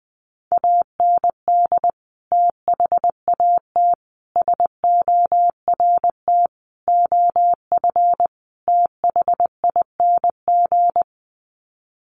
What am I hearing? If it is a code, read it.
Morse code, heard as a keyed tone, about 20 wpm, 700 Hz. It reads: AND THAT SORT OF THING
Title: Sets of 5 Words Encore2 20wpm